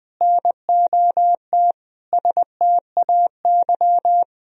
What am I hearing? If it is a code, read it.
Morse code, heard as a keyed tone, about 20 wpm, 700 Hz. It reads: NOT STAY
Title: Sentences from Top 1000 Words - Review Part 2 - 20wpm